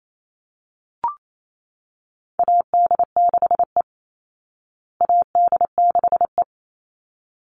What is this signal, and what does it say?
Morse code, heard as a keyed tone, about 28 wpm, 700 Hz. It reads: AD6E AD6E
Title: Call Signs Repeated 3x - 28wpm